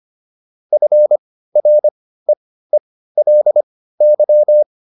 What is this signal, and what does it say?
Morse code, heard as a keyed tone, about 25 wpm, spaced as for 15 wpm, 600 Hz. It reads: FREELY